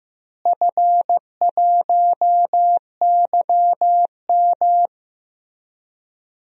Morse code, heard as a keyed tone, about 15 wpm, 700 Hz. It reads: F1YM
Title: DXCC-100 Call Signs - 4 Characters - Repeated 2x - 15wpm